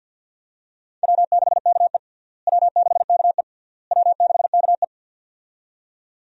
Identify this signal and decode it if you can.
Morse code, heard as a keyed tone, about 52 wpm, 700 Hz. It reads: W6KE W6KE W6KE
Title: USA Call Signs - 4 Characters - Repeated 3x - 50wpm